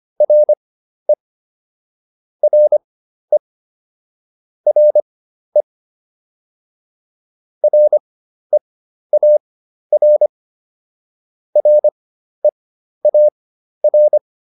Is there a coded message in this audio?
Yes: Morse code from 0.2 to 14.2 s, heard as a keyed tone, about 25 wpm, spaced as for 12 wpm, 600 Hz. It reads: RE RE RE REAR REAR